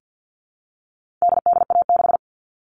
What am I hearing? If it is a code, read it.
Morse code, heard as a keyed tone, about 50 wpm, 700 Hz. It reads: DDA6
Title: All Cut Numbers - 4-Digit 50wpm